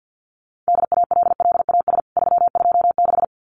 Morse code, heard as a keyed tone, about 50 wpm, 700 Hz. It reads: DALLAS 326
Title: CWT Exchange 50wpm